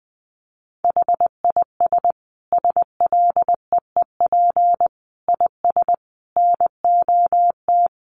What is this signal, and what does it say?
Morse code, heard as a keyed tone, about 19 wpm, 700 Hz. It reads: HIS SLEEP IS NOT
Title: Sentences from Top 700 Words 20wpm